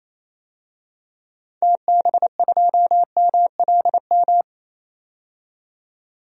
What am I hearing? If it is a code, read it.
Morse code, heard as a keyed tone, about 27 wpm, 700 Hz. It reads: TB2MLM
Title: DXCC-100 Call Signs - 6 Characters - 28wpm